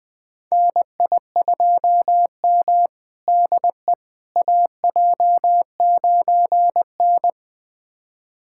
Morse code, heard as a keyed tone, about 20 wpm, 700 Hz. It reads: NI2M DE AJ9N
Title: USA Callsign DE Callsign - 4 Characters - 20wpm